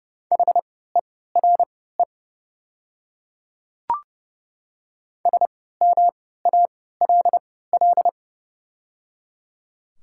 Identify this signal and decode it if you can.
Morse code, heard as a keyed tone, about 31 wpm, spaced as for 17 wpm, 690 Hz. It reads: HERE SMALL
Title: 3x Character Spacing - Top 200 Words - 30wpm